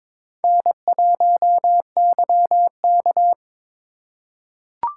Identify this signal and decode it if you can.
Morse code, heard as a keyed tone, about 21 wpm, 700 Hz. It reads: N1YK
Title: USA Call Signs - 4 Characters - 4x Character Spacing - 22wpm